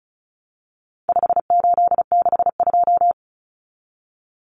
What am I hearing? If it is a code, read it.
Morse code, heard as a keyed tone, about 34 wpm, 700 Hz. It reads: 5862
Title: Common Cut Numbers - 4-Digit 35wpm